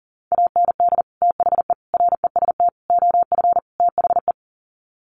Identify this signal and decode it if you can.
Morse code, heard as a keyed tone, about 40 wpm, 700 Hz. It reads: AND THE REST OF THE
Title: Sets of 5 Words 40wpm